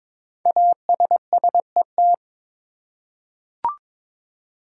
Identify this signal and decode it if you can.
Morse code, heard as a keyed tone, about 22 wpm, 700 Hz. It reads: ASSET
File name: 5 Letter Words - First 3 Letters Given - 22wpm